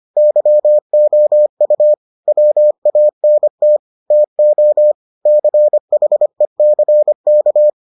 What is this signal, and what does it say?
Morse code, heard as a keyed tone, about 25 wpm, 600 Hz. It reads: YOU WANT TO CHECK